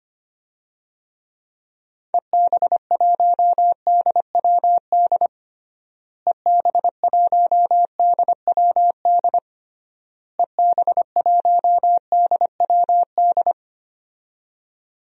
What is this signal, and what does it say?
Morse code, heard as a keyed tone, about 25 wpm, 700 Hz. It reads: EB1DWD EB1DWD EB1DWD
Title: DXCC-100 Call Signs - 6 Characters - Repeated 3x - 25wpm